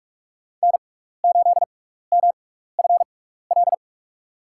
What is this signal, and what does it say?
Morse code, heard as a keyed tone, about 46 wpm, spaced as for 16 wpm, 700 Hz. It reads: N8MFL
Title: USA Call Signs - 5 Characters - 6x Character Spacing - 45wpm